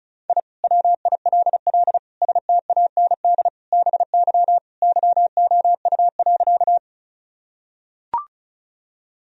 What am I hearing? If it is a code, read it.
Morse code, heard as a keyed tone, about 34 wpm, 700 Hz. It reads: I WILL STAND BY YOU.
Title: Sentences from Top 1000 Words - Review Part 1 - 35wpm